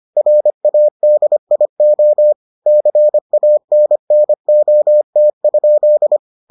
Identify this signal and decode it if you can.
Morse code, heard as a keyed tone, about 25 wpm, 600 Hz. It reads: RADIO CANNOT?